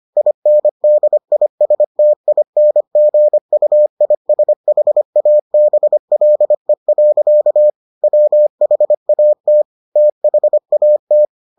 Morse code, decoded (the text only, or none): INDISTINGUISHABLE. WHAT THAT